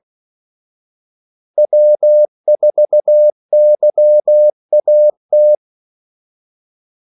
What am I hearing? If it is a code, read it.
Morse code, heard as a keyed tone, about 16 wpm, 600 Hz. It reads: W4YAT
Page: Kaldesignaler 16 wpm